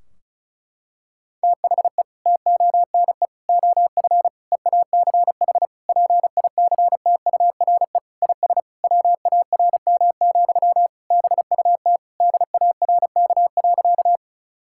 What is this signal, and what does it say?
Morse code, heard as a keyed tone, about 36 wpm, 700 Hz. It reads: THE TONE OF EACH PICTURE IS WARM, BUT DARK.
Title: Sentences from Top 1600 Words 35wpm